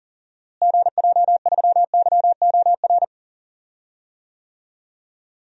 Morse code, heard as a keyed tone, about 40 wpm, 700 Hz. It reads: GJ3YOR
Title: DXCC-100 Call Signs - 6 Characters - Repeated 3x - 40wpm